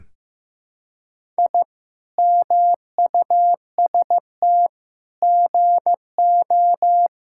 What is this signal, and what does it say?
Morse code, heard as a keyed tone, about 16 wpm, 700 Hz. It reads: I MUST GO
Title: Sentences from Top 1000 Words - Review Part 5 - 15wpm